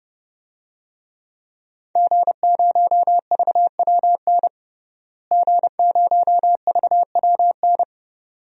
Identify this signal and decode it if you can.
Morse code, heard as a keyed tone, about 30 wpm, 700 Hz. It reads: G0VWN G0VWN
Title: DXCC-100 Call Signs - 5 Characters - Repeated 2x - 30wpm